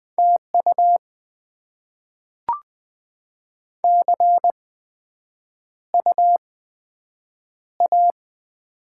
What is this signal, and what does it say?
Morse code, heard as a keyed tone, about 20 wpm, spaced as for 9 wpm, 700 Hz. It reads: TU CU A
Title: Top 20 QSO Elements - 8x Character Spacing - 20wpm